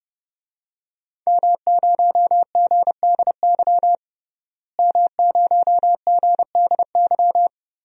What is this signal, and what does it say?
Morse code, heard as a keyed tone, about 30 wpm, 700 Hz. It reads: M0GDY M0GDY
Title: DXCC-100 Call Signs - 5 Characters - Repeated 3x - 30wpm